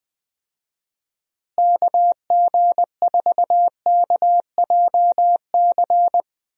Morse code, heard as a keyed tone, about 21 wpm, 700 Hz. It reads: KG4KJC
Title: USA Call Signs - 6 Characters - Repeated 2x - 20wpm